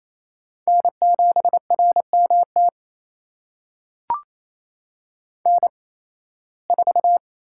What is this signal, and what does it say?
Morse code, heard as a keyed tone, about 28 wpm, spaced as for 18 wpm, 700 Hz. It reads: N7RMT N4
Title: USA Call Signs - 5 Characters - 8x Character Spacing - 28wpm